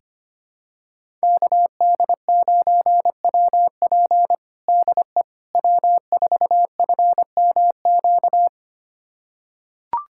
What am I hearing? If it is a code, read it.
Morse code, heard as a keyed tone, about 25 wpm, 700 Hz. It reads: KD9WP DE W4FMQ
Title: USA Callsign DE Callsign - 5 Characters - 25wpm